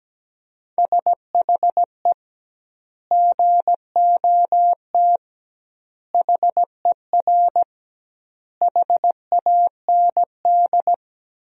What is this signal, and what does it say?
Morse code, heard as a keyed tone, about 17 wpm, 700 Hz. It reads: SHE GOT HER HAND
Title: 2x Word-Spacing - Sentences from Top 500 Words - 17wpm